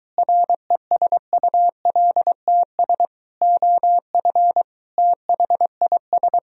Morse code, heard as a keyed tone, about 23 wpm, 700 Hz. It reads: RESULTS OF THIS